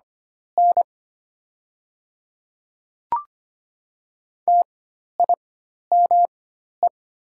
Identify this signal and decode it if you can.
Morse code, heard as a keyed tone, about 25 wpm, spaced as for 12 wpm, 700 Hz. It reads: N TIME
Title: 4x Character Spacing - Top 100 Words - 25wpm